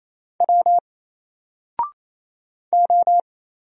Morse code, heard as a keyed tone, about 28 wpm, 700 Hz. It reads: W O
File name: Single-Letter-Number-28wpm